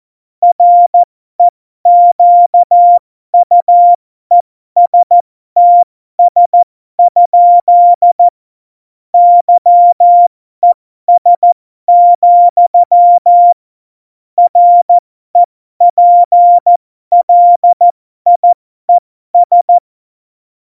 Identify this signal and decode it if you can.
Morse code, heard as a keyed tone, about 14 wpm, spaced as for 12 wpm, 700 Hz. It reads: REQUESTS? YES, REPLIES
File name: Morsekurs / Witze